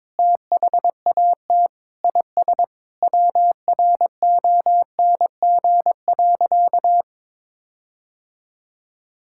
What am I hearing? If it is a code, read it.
Morse code, heard as a keyed tone, about 22 wpm, 700 Hz. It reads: THAT IS WRONG.
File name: practice / 22wpm / Sentences with 4-Word Phrases - 22wpm